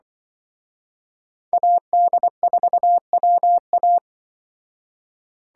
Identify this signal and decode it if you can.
Morse code, heard as a keyed tone, about 24 wpm, 700 Hz. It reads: AD4WA
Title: Speed Racing - USA 2x2 Call Signs 20wpm